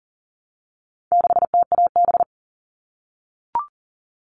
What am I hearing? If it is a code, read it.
Morse code, heard as a keyed tone, about 40 wpm, 700 Hz. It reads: 6TAB
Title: All Cut Numbers - 4-Digit 40wpm